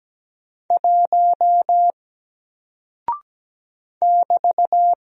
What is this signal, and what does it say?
Morse code, heard as a keyed tone, about 17 wpm, 700 Hz. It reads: 1 <BT>
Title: Single Character (Letter:Number:Punctuation:Prosign) 17wpm